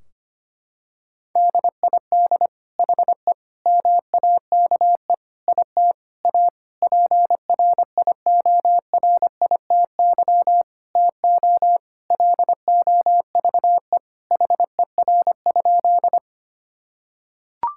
Morse code, heard as a keyed tone, about 25 wpm, 700 Hz. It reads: DID HE MAKE IT A PRIORITY TO LOVE HER?
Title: Sentences from Top 1600 Words 25wpm